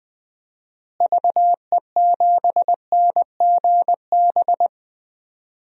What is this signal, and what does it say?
Morse code, heard as a keyed tone, about 20 wpm, 700 Hz. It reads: VE7NGB
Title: DXCC-100 Call Signs - 6 Characters - Repeated 2x - 20wpm